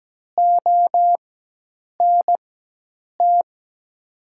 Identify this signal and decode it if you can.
Morse code, heard as a keyed tone, about 17 wpm, spaced as for 8 wpm, 700 Hz. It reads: ONT
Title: 4x Character Spacing - Top 300 Words - 17wpm